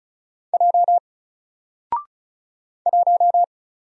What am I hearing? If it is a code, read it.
Morse code, heard as a keyed tone, about 34 wpm, 700 Hz. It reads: J 1
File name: Single-Letter-Number-35wpm